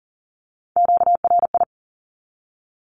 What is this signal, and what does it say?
Morse code, heard as a keyed tone, about 38 wpm, 700 Hz. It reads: QRI
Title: Q-Codes Repeated 3x 40wpm